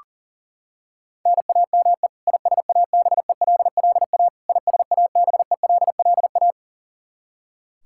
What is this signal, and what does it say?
Morse code, heard as a keyed tone, about 40 wpm, 700 Hz. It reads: NAME ISABELLA ISABELLA
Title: Practice QSO - Exchange 1 - Beginning - 40wpm